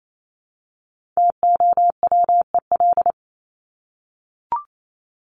Morse code, heard as a keyed tone, about 28 wpm, 700 Hz. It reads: TOWEL
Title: Top 1501-1750 Words 28wpm